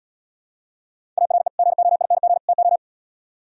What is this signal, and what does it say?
Morse code, heard as a keyed tone, about 76 wpm, 700 Hz. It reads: ARE GOING TO